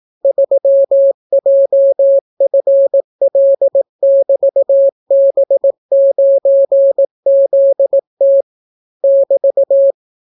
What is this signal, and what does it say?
Morse code, heard as a keyed tone, about 18 wpm, 550 Hz. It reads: 3JFL<BT>B9ZT <BT>